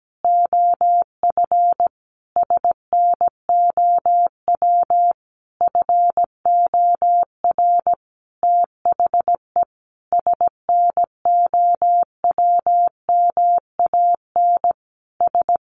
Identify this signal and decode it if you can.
Morse code, heard as a keyed tone, about 17 wpm, 700 Hz. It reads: OF SNOW FOR THE SNOWMAN S
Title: Easy Phrases 17wpm